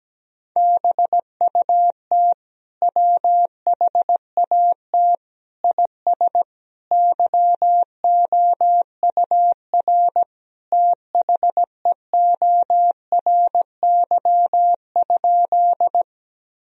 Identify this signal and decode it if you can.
Morse code, heard as a keyed tone, about 17 wpm, 700 Hz. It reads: BUT WHAT IS YOUR THEORY?
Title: Sentences from Top 1000 Words - Review Part 3 - 17wpm